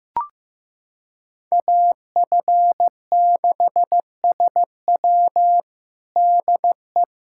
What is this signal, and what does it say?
Morse code, heard as a keyed tone, about 15 wpm, 700 Hz. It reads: AF6SW DE
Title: USA Callsign DE Callsign - 5 Characters - 15wpm